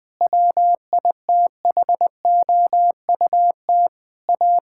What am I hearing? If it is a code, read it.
Morse code, heard as a keyed tone, about 20 wpm, 700 Hz. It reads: WITHOUT A
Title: Sentences from Top 1600 Words 20wpm